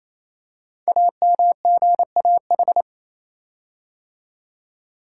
Morse code, heard as a keyed tone, about 28 wpm, 700 Hz. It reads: AMGAH
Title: Code Groups - 5 Letters - 28wpm